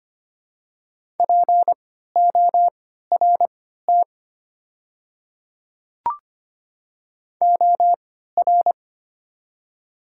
Morse code, heard as a keyed tone, about 25 wpm, spaced as for 14 wpm, 700 Hz. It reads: PORT OR
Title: 3x Character Spacing - Top 200 Words - 25wpm